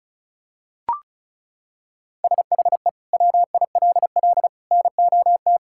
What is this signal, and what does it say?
Morse code, heard as a keyed tone, about 36 wpm, 700 Hz. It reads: SHE WILL NOT
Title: Sentences from Top 1000 Words - Review Part 6 - 35wpm